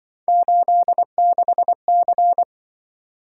Morse code, heard as a keyed tone, about 25 wpm, 700 Hz. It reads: 86C
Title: Speed-Racing - Three Letter-Number Combinations - 20wpm